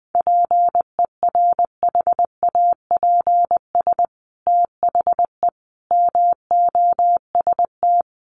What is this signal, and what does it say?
Morse code, heard as a keyed tone, about 20 wpm, 700 Hz. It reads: PERHAPS THE MOST
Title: Sentences from Top 1300 Words 20wpm